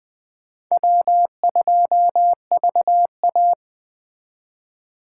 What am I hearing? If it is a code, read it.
Morse code, heard as a keyed tone, about 20 wpm, 700 Hz. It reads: W2VA